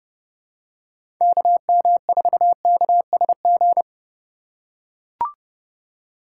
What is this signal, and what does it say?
Morse code, heard as a keyed tone, about 30 wpm, 700 Hz. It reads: KM4KSG
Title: USA Call Signs - 6 Characters - 4x Character Spacing - 30wpm